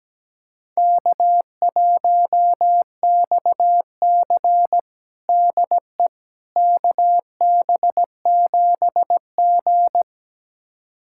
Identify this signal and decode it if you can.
Morse code, heard as a keyed tone, about 17 wpm, 700 Hz. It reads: K1XC DE KB7G
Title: USA Callsign DE Callsign - 4 Characters - 17wpm